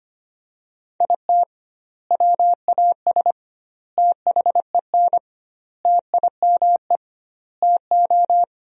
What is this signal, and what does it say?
Morse code, heard as a keyed tone, about 25 wpm, 700 Hz. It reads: IT WAS THEN TIME TO
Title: 2x Word-Spacing - Sentences from Top 300 Words - 25wpm